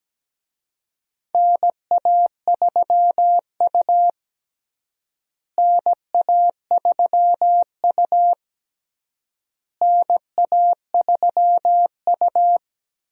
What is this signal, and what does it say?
Morse code, heard as a keyed tone, about 17 wpm, 700 Hz. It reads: NA3U NA3U NA3U
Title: USA Call Signs - 4 Characters - Repeated 3x - 17wpm